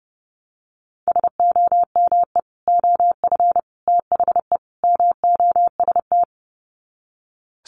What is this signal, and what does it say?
Morse code, heard as a keyed tone, about 30 wpm, 700 Hz. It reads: SOME OF THE MOST
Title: Sets of 4 Words 30wpm